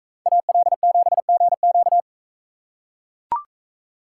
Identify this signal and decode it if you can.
Morse code, heard as a keyed tone, about 42 wpm, 700 Hz. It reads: AL7GQ